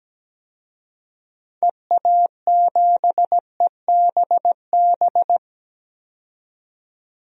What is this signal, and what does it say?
Morse code, heard as a keyed tone, about 17 wpm, 700 Hz. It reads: EA7EBB
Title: DXCC-100 Call Signs - 6 Characters - 17wpm